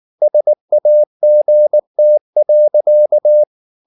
Morse code, heard as a keyed tone, about 19 wpm, 600 Hz. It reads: SAGT.